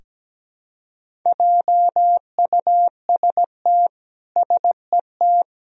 Morse code, heard as a keyed tone, about 17 wpm, 700 Hz. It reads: JUST SET
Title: Sentences from Top 300 Words and Abbreviations 17wpm